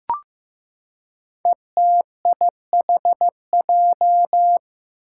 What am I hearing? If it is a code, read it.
Morse code, heard as a keyed tone, about 15 wpm, 700 Hz. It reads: ETIHJ
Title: Five Challenging Letter-Number Combinations 15wpm